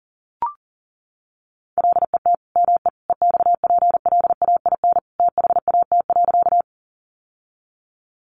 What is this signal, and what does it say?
Morse code, heard as a keyed tone, about 40 wpm, 700 Hz. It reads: LET ME EXPLAIN THAT.
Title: Sentences from Top 1100 Words 40wpm